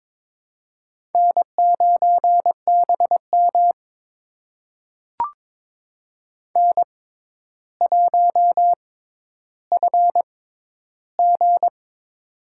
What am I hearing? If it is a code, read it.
Morse code, heard as a keyed tone, about 22 wpm, spaced as for 11 wpm, 700 Hz. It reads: N9BM N1FG
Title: USA Call Signs - 4 Characters - 6x Character Spacing - 22wpm